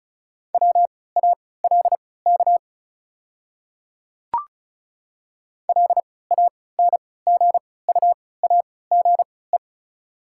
Morse code, heard as a keyed tone, about 34 wpm, spaced as for 20 wpm, 690 Hz. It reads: WALK LANGUAGE
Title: 3x Character Spacing - Top 500 Words - 35wpm